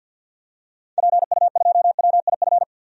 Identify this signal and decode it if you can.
Morse code, heard as a keyed tone, about 52 wpm, 700 Hz. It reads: PU2WIF
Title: DXCC-100 Call Signs - 6 Characters - 50wpm